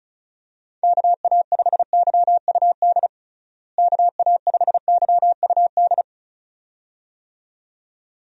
Morse code, heard as a keyed tone, about 34 wpm, 700 Hz. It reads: KA5YUD KA5YUD
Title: USA Call Signs - 6 Characters - Repeated 2x - 35wpm